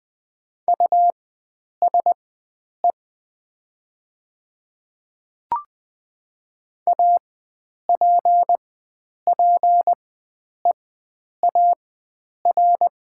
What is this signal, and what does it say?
Morse code, heard as a keyed tone, about 20 wpm, spaced as for 9 wpm, 700 Hz. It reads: USE APPEAR
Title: 4x Character Spacing - Top 500 Words - 20wpm